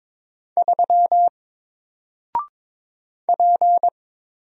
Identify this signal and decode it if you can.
Morse code, heard as a keyed tone, about 22 wpm, 700 Hz. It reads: 3 P
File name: Single-Letter-Number-22wpm